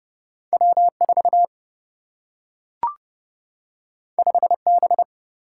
Morse code, heard as a keyed tone, about 30 wpm, 700 Hz. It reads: W 4 5 B
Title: Single-Letter Single-Number 30wpm